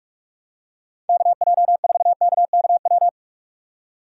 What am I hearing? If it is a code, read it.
Morse code, heard as a keyed tone, about 47 wpm, 700 Hz. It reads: KJ4KKW
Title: USA Call Signs - 6 Characters - 6x Character Spacing - 45wpm